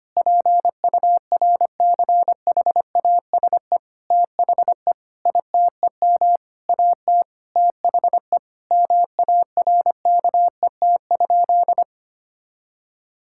Morse code, heard as a keyed tone, about 25 wpm, 700 Hz. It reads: PURCHASE THE ITEM AT THE MARKET?
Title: Sentences from Top 1000 Words - Review Part 6 - 25wpm